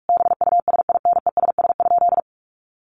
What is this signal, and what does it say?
Morse code, heard as a keyed tone, about 44 wpm, 700 Hz. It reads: BUSINESS?
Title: Sentences from Top 1300 Words 45wpm